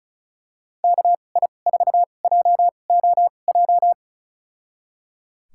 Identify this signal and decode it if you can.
Morse code, heard as a keyed tone, about 36 wpm, spaced as for 25 wpm, 700 Hz. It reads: KI4JOJ
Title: USA Call Signs - 2x Character Spacing - 35wpm